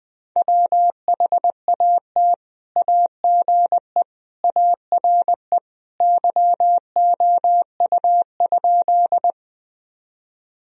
Morse code, heard as a keyed tone, about 20 wpm, 700 Hz. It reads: WHAT AGE ARE YOU?
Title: Sentences from Top 1000 Words - Review Part 3 - 20wpm